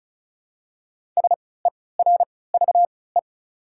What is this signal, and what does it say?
Morse code, heard as a keyed tone, about 34 wpm, spaced as for 20 wpm, 700 Hz. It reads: SERVE
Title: 3x Character Spacing - Top 500 Words - 35wpm